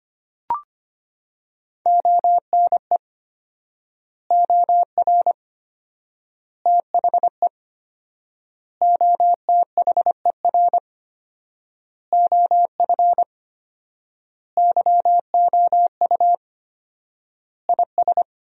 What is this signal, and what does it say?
Morse code, heard as a keyed tone, about 25 wpm, 700 Hz. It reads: ONE OR THE OTHER OF YOU IS
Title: 4x Word-Spacing - Sentences from Top 400 Words - 25wpm